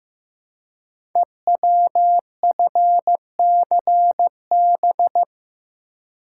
Morse code, heard as a keyed tone, about 15 wpm, 700 Hz. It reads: EWFCB
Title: Code Groups - 5 Letters - 15wpm